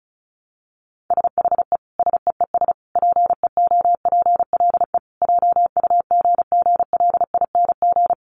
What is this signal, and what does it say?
Morse code, heard as a keyed tone, about 34 wpm, 700 Hz. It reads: SHE SEES PEOPLE JUGGLING
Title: Easy Phrases 35wpm